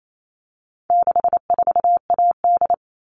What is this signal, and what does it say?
Morse code, heard as a keyed tone, about 28 wpm, 700 Hz. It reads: 64AD
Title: All Cut Numbers - 4-Digit 28wpm